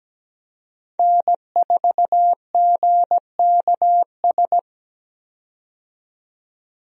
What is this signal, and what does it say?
Morse code, heard as a keyed tone, about 17 wpm, 700 Hz. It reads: N4GKS
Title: USA Call Signs - 5 Characters - 17wpm